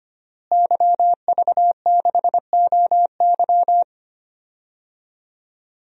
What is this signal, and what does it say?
Morse code, heard as a keyed tone, about 25 wpm, 700 Hz. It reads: YV6OY
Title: DXCC-100 Call Signs - 5 Characters - Repeated 2x - 25wpm